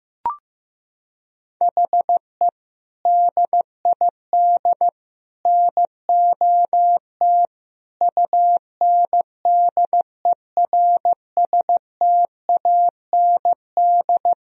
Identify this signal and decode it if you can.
Morse code, heard as a keyed tone, about 15 wpm, 700 Hz. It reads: HE DID NOT UNDERSTAND
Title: Sentences from Top 1000 Words - Review Part 3 - 15wpm